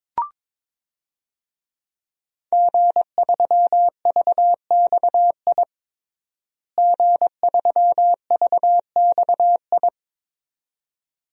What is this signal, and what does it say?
Morse code, heard as a keyed tone, about 22 wpm, 700 Hz. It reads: G3VXI G3VXI
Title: DXCC-100 Call Signs - 5 Characters - Repeated 2x - 22wpm